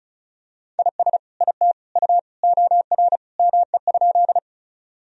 Morse code, heard as a keyed tone, about 36 wpm, 700 Hz. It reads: IS IT U OR ME?
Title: Sentences from Top 200 Words and Abbreviations 35wpm